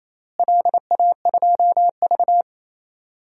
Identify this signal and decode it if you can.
Morse code, heard as a keyed tone, about 28 wpm, 700 Hz. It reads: LA2V
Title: DXCC-100 Call Signs - 4 Characters - Repeated 3x - 28wpm